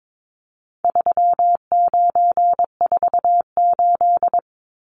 Morse code, heard as a keyed tone, about 22 wpm, 700 Hz. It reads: 3948
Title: Numbers - 4-Digit 22wpm